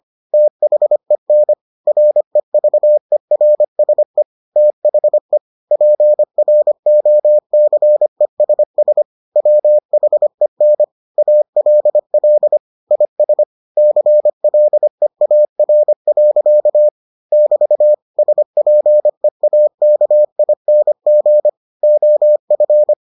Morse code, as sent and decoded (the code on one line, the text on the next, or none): - .... -. / .-. ...- . .-. ... / - .... / .--. .-. --- -.-. ... ... / .-- .... -. / .- .-.. .-.. / .. ... / -.-. .-.. .- .-. .-.-.- / -...- / ... .--. .- -.- .. -. --. / --- ..-.
THEN REVERSE THE PROCESS WHEN ALL IS CLEAR. <BT> SPEAKING OF